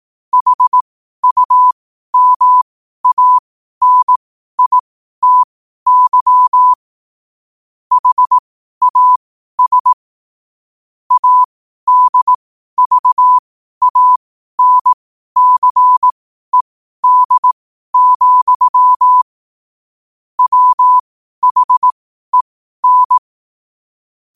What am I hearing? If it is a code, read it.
Morse code, heard as a keyed tone, about 18 wpm, spaced as for 12 wpm, 1000 Hz. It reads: HUMANITY HAS ADVANCED, WHEN
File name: qotdfiles / 12wpm